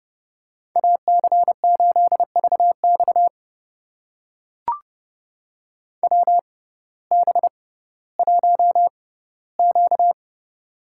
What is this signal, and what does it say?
Morse code, heard as a keyed tone, about 31 wpm, spaced as for 17 wpm, 700 Hz. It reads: AC8VX WB1Q
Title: USA Call Signs - 6x Character Spacing - 30wpm